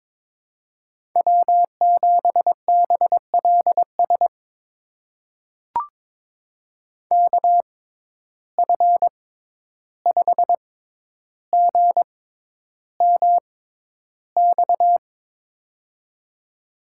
Morse code, heard as a keyed tone, about 22 wpm, spaced as for 11 wpm, 700 Hz. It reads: W7BLS KF5GMX
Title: USA Call Signs - 6x Character Spacing - 22wpm